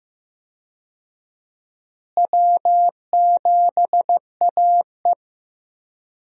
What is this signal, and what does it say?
Morse code, heard as a keyed tone, about 15 wpm, 700 Hz. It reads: W7AE
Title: USA Call Signs - 4 Characters - Repeated 2x - 15wpm